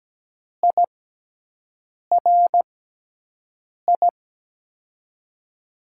Morse code, heard as a keyed tone, about 17 wpm, spaced as for 6 wpm, 700 Hz. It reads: IRI
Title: USA Call Signs - 6x Character Spacing - 17wpm